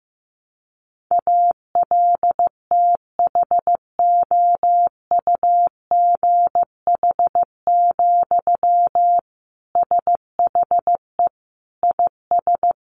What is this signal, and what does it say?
Morse code, heard as a keyed tone, about 15 wpm, 700 Hz. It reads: ALTHOUGH, SHE IS
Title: Sentences from Top 600 Words 15wpm